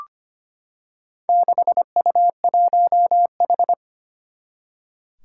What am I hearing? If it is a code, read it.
Morse code, heard as a keyed tone, about 25 wpm, 700 Hz. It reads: 6U1H
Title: Four Challenging Letter-Number Combinations 25wpm